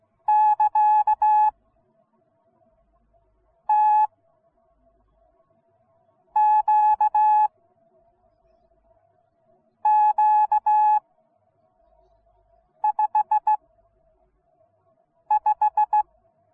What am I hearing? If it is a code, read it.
Morse code, heard as a keyed tone, about 16 wpm, 850 Hz. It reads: <KA> T Q Q 5 5